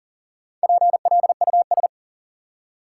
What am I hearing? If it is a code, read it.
Morse code, heard as a keyed tone, about 40 wpm, 700 Hz. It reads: PLUS
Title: Four Letter Words - 40wpm